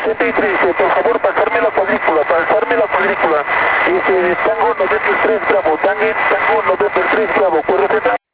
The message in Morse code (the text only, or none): none